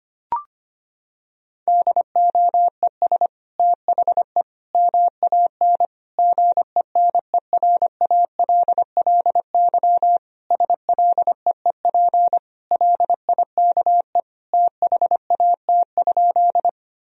Morse code, heard as a keyed tone, about 25 wpm, 700 Hz. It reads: DOES THE MAN GENERALLY SLEEP LIKE THAT?
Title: Sentences from Top 1000 Words - Review Part 4 - 25wpm